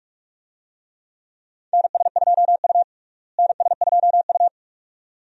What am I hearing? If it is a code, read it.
Morse code, heard as a keyed tone, about 42 wpm, 700 Hz. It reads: NS2U NS2U
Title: USA Call Signs - 4 Characters - Repeated 2x - 45wpm